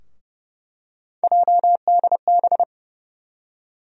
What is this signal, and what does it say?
Morse code, heard as a keyed tone, about 30 wpm, 700 Hz. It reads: JDB
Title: USA Call Sign Postfixes - 3 Letters 30wpm